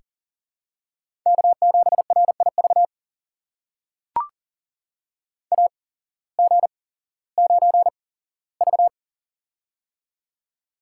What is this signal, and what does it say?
Morse code, heard as a keyed tone, about 42 wpm, spaced as for 19 wpm, 700 Hz. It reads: K7RIV AG9V
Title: USA Call Signs - 8x Character Spacing - 40wpm